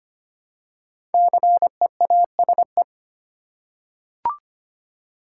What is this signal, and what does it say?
Morse code, heard as a keyed tone, about 25 wpm, 700 Hz. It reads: CEASE
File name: 5 Letter Words - First 3 Letters Given - 25wpm